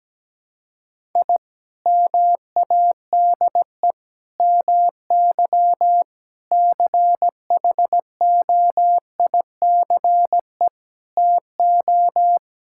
Morse code, heard as a keyed tone, about 17 wpm, 700 Hz. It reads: I MADE MY CHOICE TO
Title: Sentences from Top 1000 Words - Review Part 6 - 17wpm